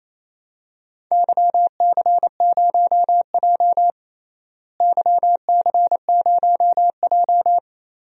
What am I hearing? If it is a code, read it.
Morse code, heard as a keyed tone, about 28 wpm, 700 Hz. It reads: YC0J YC0J
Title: DXCC-100 Call Signs - 4 Characters - Repeated 3x - 28wpm